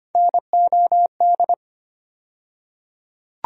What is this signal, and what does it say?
Morse code, heard as a keyed tone, about 25 wpm, 700 Hz. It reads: NOD
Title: USA Call Sign Postfixes - 3 Letters 25wpm